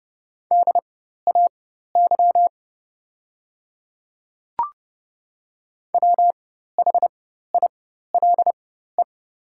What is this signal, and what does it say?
Morse code, heard as a keyed tone, about 30 wpm, spaced as for 14 wpm, 700 Hz. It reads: DAY WHILE
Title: 4x Character Spacing - Top 300 Words - 30wpm